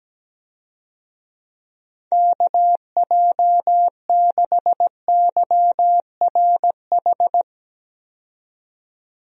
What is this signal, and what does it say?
Morse code, heard as a keyed tone, about 17 wpm, 700 Hz. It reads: KJ6YRH
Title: USA Call Signs - 6 Characters - 17wpm